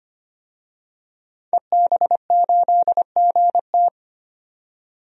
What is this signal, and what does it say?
Morse code, heard as a keyed tone, about 25 wpm, 700 Hz. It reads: EB8GT
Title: DXCC-100 Call Signs - 5 Characters - 25wpm